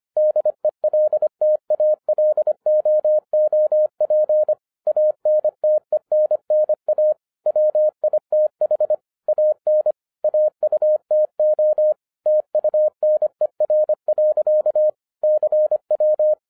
Morse code, heard as a keyed tone, about 25 wpm, 610 Hz. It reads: DELTALOOP ANTENNA WITH AN AUTO TUNER. CW